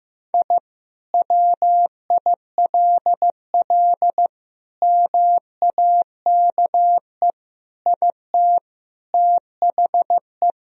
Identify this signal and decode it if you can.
Morse code, heard as a keyed tone, about 15 wpm, 700 Hz. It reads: I WILL MAKE IT THE
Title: Sentences from Top 1000 Words - Review Part 3 - 15wpm